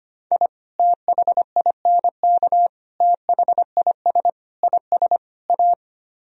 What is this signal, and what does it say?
Morse code, heard as a keyed tone, about 32 wpm, 700 Hz. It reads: I THINK THIS IS A